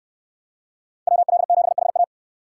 Morse code, heard as a keyed tone, about 68 wpm, 700 Hz. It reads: WL7HA